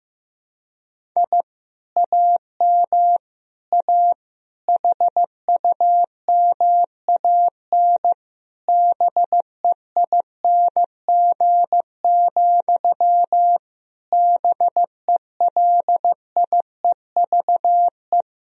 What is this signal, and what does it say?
Morse code, heard as a keyed tone, about 16 wpm, 700 Hz. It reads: I AM A HUMAN BEING, BELIEVE
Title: Sentences from Top 1000 Words - Review Part 3 - 15wpm